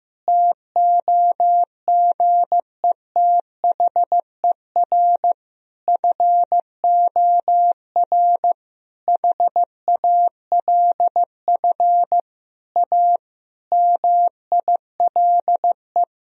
Morse code, heard as a keyed tone, about 14 wpm, 700 Hz. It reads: TOGETHER FOR HALF A MILE
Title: Sentences from Top 1000 Words - Review Part 1 - 15wpm